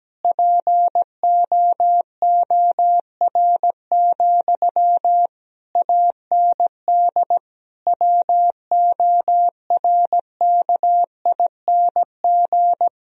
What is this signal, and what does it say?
Morse code, heard as a keyed tone, about 17 wpm, 700 Hz. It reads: POOR, AND WORKING
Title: Sentences from Top 1000 Words - Review Part 4 - 17wpm